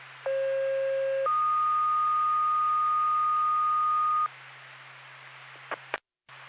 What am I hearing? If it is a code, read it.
Two-tone page, heard as a one-second tone, then a three-second tone, about 540 then 1230 Hz. It is heard from 0.3 to 4.3 s.